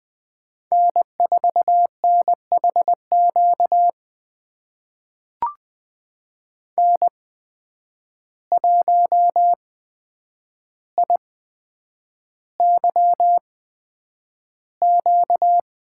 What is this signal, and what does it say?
Morse code, heard as a keyed tone, about 20 wpm, spaced as for 9 wpm, 700 Hz. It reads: N4NHQ N1IYQ
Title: USA Call Signs - 5 Characters - 8x Character Spacing - 20wpm